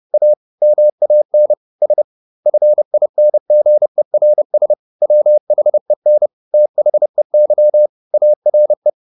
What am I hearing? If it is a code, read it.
Morse code, heard as a keyed tone, about 30 wpm, 600 Hz. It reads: A MAN S FINGERS WHEN THEY ARE